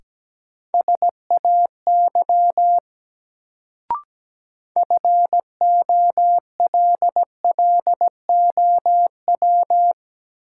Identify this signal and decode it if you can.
Morse code, heard as a keyed tone, about 17 wpm, 700 Hz. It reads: SAY FOLLOW
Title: Sentences from Top 200 Words - New Word Drill - 17wpm